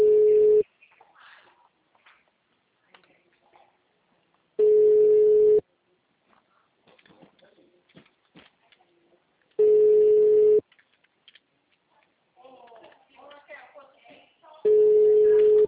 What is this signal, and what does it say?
Ringback tone (Europe, 425 Hz), with speech.